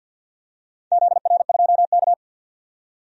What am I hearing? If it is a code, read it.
Morse code, heard as a keyed tone, about 48 wpm, 700 Hz. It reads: ZR2K
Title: DXCC-100 Call Signs - 4 Characters - 50wpm